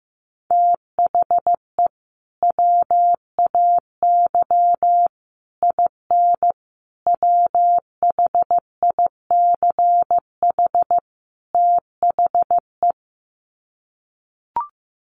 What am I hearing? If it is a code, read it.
Morse code, heard as a keyed tone, about 15 wpm, 700 Hz. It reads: THE WAY IN WHICH THE
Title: Sets of 5 Words Encore2 15wpm